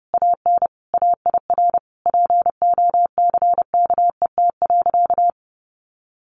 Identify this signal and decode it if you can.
Morse code, heard as a keyed tone, about 30 wpm, 700 Hz. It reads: AN AIR POCKET.
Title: A Sound Of Thunder 30wpm